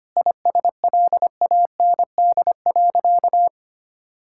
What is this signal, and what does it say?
Morse code, heard as a keyed tone, about 25 wpm, 700 Hz. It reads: ISLAND.
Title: Sentences from Top 1600 Words 25wpm